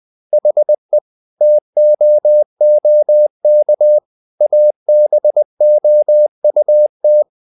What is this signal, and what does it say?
Morse code, heard as a keyed tone, about 20 wpm, 600 Hz. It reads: HE TOOK ABOUT
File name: the old man and the sea part 2 20wpm